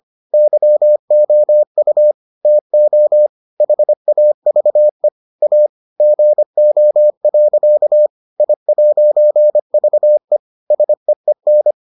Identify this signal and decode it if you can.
Morse code, heard as a keyed tone, about 25 wpm, 600 Hz. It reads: YOU TO HAVE A GO. I'VE SEEN